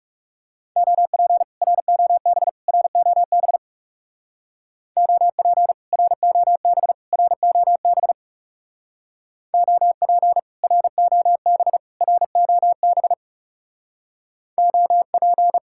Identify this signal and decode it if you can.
Morse code, heard as a keyed tone, about 35 wpm, 700 Hz. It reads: OP ROB ROB OP ROB ROB OP ROB ROB OP
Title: Speed-Racing Common QSO Phrases - 30wpm